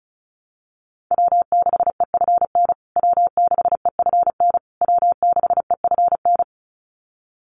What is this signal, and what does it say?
Morse code, heard as a keyed tone, about 34 wpm, 700 Hz. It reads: W6EFN W6EFN W6EFN
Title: USA Call Signs Repeated 3x 35wpm